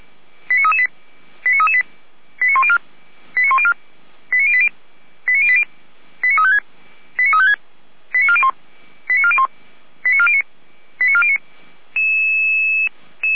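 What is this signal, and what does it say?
Five-tone selective call (ZVEI), heard as quick runs of five tones, tones of about 70 ms. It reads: 89338 89338 89104 89104 89080 89080 89356 89356 89401 89401 89409 89409
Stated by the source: ZVEI five-tone selective call received on SDR.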